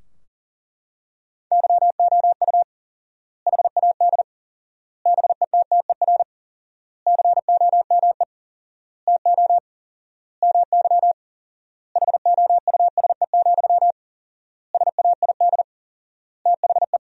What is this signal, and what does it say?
Morse code, heard as a keyed tone, about 40 wpm, 700 Hz. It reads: YOU HAD BETTER COME TO MY HOUSE, SAID THE
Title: 4x Word-Spacing - Sentences from Top 400 Words - 40wpm